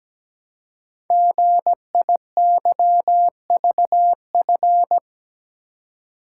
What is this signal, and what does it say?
Morse code, heard as a keyed tone, about 17 wpm, 700 Hz. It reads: GIYVF
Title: Code Groups - 5 Letters - 17wpm